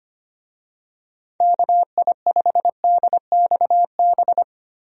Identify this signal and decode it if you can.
Morse code, heard as a keyed tone, about 25 wpm, 700 Hz. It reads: KI5DXB
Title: USA Call Signs - 6 Characters - 25wpm